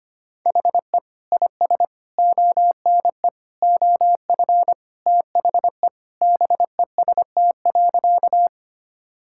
Morse code, heard as a keyed tone, about 25 wpm, 700 Hz. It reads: HE IS ONE OF THE BEST.
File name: practice / 25wpm / Sentences with 4-Word Phrases - 25wpm